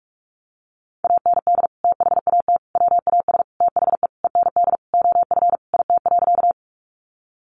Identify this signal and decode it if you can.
Morse code, heard as a keyed tone, about 44 wpm, 700 Hz. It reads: AND THAT WAS THE END OF IT.
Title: Sentences from Top 200 Words 45wpm